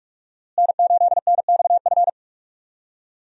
Morse code, heard as a keyed tone, about 45 wpm, 700 Hz. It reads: N8NXF
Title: USA Call Signs - 5 Characters - Repeated 2x - 45wpm